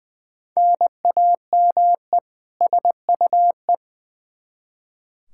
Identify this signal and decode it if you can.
Morse code, heard as a keyed tone, about 21 wpm, 700 Hz. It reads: NAME SUE
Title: Practice QSO - Exchange 1 - Beginning - 20wpm